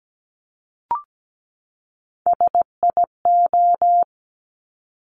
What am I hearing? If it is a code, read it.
Morse code, heard as a keyed tone, about 17 wpm, 700 Hz. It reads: SIO
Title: Top Three Letter Combinations in Words 17wpm